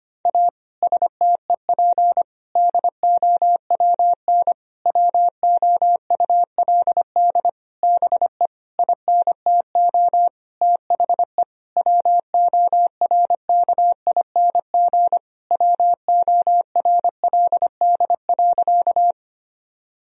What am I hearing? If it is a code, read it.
Morse code, heard as a keyed tone, about 25 wpm, 700 Hz. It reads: A STEP DOWN WOULD BE INTO THE WORKING WORLD.
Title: Sentences from Top 1000 Words - Review Part 4 - 25wpm